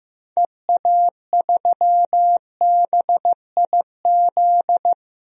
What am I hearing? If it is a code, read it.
Morse code, heard as a keyed tone, about 15 wpm, 700 Hz. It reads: EA3BIZ
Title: DXCC-100 Call Signs - 6 Characters - 15wpm